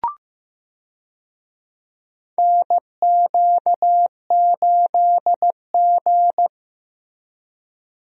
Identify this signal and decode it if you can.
Morse code, heard as a keyed tone, about 15 wpm, 700 Hz. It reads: NQ8G
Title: USA Call Signs - 4 Characters - Repeated 2x - 15wpm